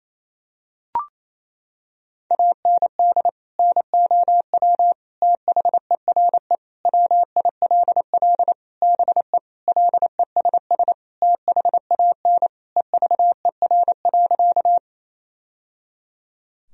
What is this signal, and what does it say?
Morse code, heard as a keyed tone, about 28 wpm, 700 Hz. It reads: AND NOW THERE WILL BE LESS THAN EVER.
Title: Sentences from Top 1000 Words - Review Part 2 - 28wpm